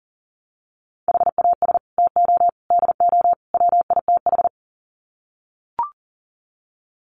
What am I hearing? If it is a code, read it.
Morse code, heard as a keyed tone, about 40 wpm, 700 Hz. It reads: HAS TO DO WITH
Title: Sets of 4 Words 40wpm